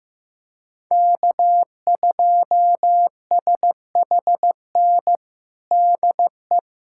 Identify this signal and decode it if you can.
Morse code, heard as a keyed tone, about 15 wpm, 700 Hz. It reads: K2SHN DE
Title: USA Callsign DE Callsign - 5 Characters - 15wpm